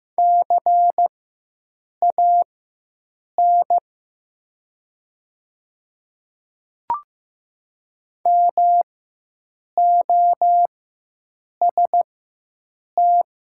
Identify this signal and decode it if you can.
Morse code, heard as a keyed tone, about 15 wpm, spaced as for 7 wpm, 700 Hz. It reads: CAN MOST
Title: 4x Character Spacing - Top 200 Words - 15wpm